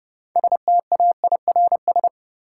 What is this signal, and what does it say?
Morse code, heard as a keyed tone, about 30 wpm, 700 Hz. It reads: STAIRS
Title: Sets of 5 Words Encore - New Word Drill - 30wpm